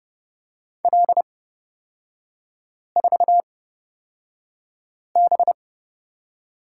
Morse code, heard as a keyed tone, about 30 wpm, spaced as for 5 wpm, 700 Hz. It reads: L4B
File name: Lesson 36e - BT - Three Characters 30wpm